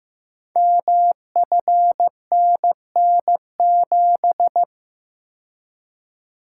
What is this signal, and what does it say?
Morse code, heard as a keyed tone, about 15 wpm, 700 Hz. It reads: MFNN7
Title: Code Groups - 5 Letter:Numbers - 15wpm